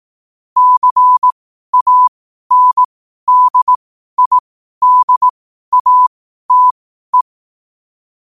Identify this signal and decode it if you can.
Morse code, heard as a keyed tone, about 18 wpm, spaced as for 12 wpm, 1000 Hz. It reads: CANDIDATE
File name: qotdfiles / 12wpm